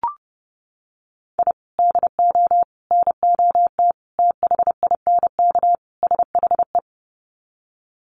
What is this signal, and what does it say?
Morse code, heard as a keyed tone, about 30 wpm, 700 Hz. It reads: I DO NOT THINK SHE
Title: Sets of 5 Words 30wpm